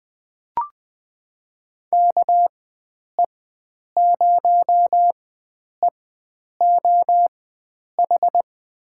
Morse code, heard as a keyed tone, about 20 wpm, spaced as for 9 wpm, 700 Hz. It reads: KE0EOH
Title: USA Call Signs - 4x Character Spacing - 20wpm